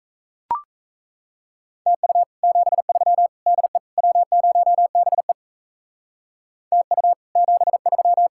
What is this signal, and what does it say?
Morse code, heard as a keyed tone, about 36 wpm, 700 Hz. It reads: TU 73 DE W0BE TU 73